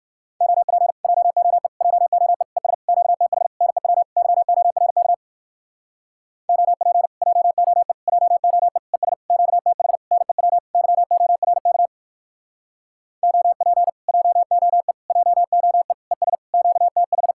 Morse code, heard as a keyed tone, about 64 wpm, 700 Hz. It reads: OP JOE JOE ES QTH NEW YORK OP JOE JOE ES QTH NEW YORK OP JOE JOE ES QTH